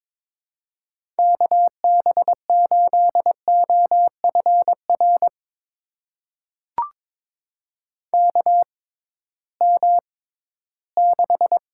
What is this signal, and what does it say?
Morse code, heard as a keyed tone, about 22 wpm, spaced as for 14 wpm, 700 Hz. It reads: KB8OFR KM6
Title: USA Call Signs - 6x Character Spacing - 22wpm